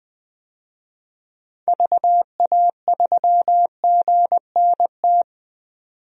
Morse code, heard as a keyed tone, about 20 wpm, 700 Hz. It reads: VA3GNT
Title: DXCC-100 Call Signs - 6 Characters - 20wpm